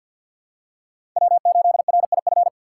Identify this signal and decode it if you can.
Morse code, heard as a keyed tone, about 46 wpm, 700 Hz. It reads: W8RIF
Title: USA Call Signs - 4x Character Spacing - 50wpm